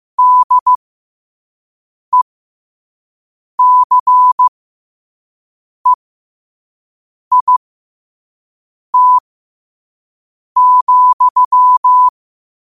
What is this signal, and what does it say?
Morse code, heard as a keyed tone, about 15 wpm, spaced as for 5 wpm, 1000 Hz. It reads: DECEIT,
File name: 5wpm